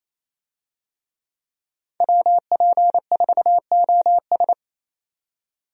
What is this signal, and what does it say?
Morse code, heard as a keyed tone, about 27 wpm, 700 Hz. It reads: WP4OS
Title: USA Call Signs - 5 Characters - 28wpm